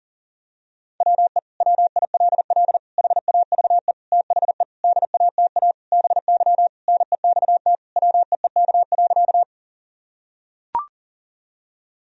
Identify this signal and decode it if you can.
Morse code, heard as a keyed tone, about 40 wpm, 700 Hz. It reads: WE WILL HAVE THE DATA BY NEXT WEEK.
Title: Sentences from Top 1000 Words - Review Part 3 - 40wpm